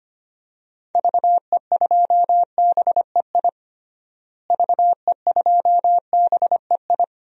Morse code, heard as a keyed tone, about 25 wpm, 700 Hz. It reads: VE2BEI VE2BEI
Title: DXCC-100 Call Signs - 6 Characters - Repeated 3x - 25wpm